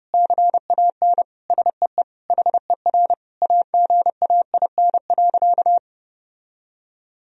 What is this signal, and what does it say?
Morse code, heard as a keyed tone, about 30 wpm, 700 Hz. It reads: CAN SEE HER AGAIN.
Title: Sentences from Top 1000 Words - Review Part 3 - 30wpm